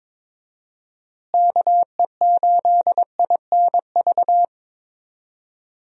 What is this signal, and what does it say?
Morse code, heard as a keyed tone, about 22 wpm, 700 Hz. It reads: KE8INV
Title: USA Call Signs - 6 Characters - Repeated 3x - 22wpm